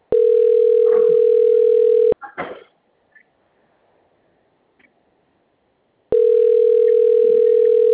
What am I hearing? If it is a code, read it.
Ringback tone (North America), 440 + 480 Hz.